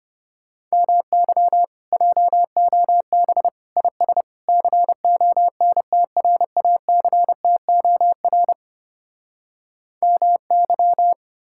Morse code, heard as a keyed tone, about 33 wpm, 700 Hz. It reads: MY JOB IS CONTRACTOR MY